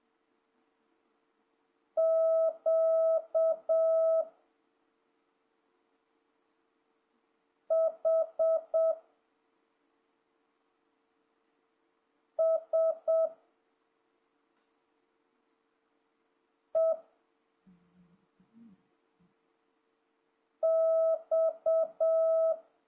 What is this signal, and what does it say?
Morse code, heard as a keyed tone, about 7 wpm, 650 Hz. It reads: Q H S E X
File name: jh8cbh / 2026cw